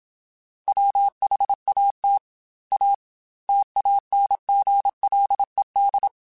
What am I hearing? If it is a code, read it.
Morse code, heard as a keyed tone, about 26 wpm, 800 Hz. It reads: WHAT A TANGLED
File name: Quotes / Quotes-mp3s / Deceive @25